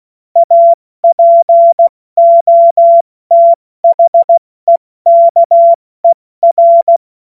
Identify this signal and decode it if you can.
Morse code, heard as a keyed tone, about 16 wpm, 680 Hz. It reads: APOTHEKER
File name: Morsekurs / Witze / Witz 05 März 2024 70 F80